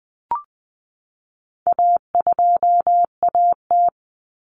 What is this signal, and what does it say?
Morse code, heard as a keyed tone, about 20 wpm, 700 Hz. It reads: A2AT
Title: Common Cut Numbers - 4-Digit 20wpm